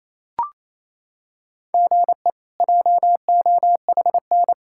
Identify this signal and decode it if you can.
Morse code, heard as a keyed tone, about 28 wpm, 700 Hz. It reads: GE JOHN
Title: K1USN SST QSO Practice - 28wpm